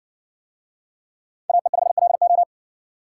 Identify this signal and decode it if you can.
Morse code, heard as a keyed tone, about 70 wpm, 700 Hz. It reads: AE5LO